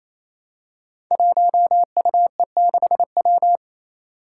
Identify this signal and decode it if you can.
Morse code, heard as a keyed tone, about 28 wpm, 700 Hz. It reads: 1UE6W
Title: Five Challenging Letter-Number Combinations 28wpm